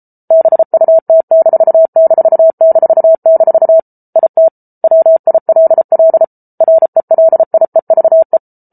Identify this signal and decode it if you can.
Morse code, heard as a keyed tone, about 34 wpm, 650 Hz. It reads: BUT---- IT WILL RELIEVE